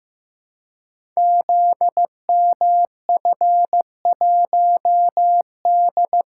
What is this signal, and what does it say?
Morse code, heard as a keyed tone, about 15 wpm, 700 Hz. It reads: ZMF1D
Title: Code Groups - 5 Letter:Numbers - 15wpm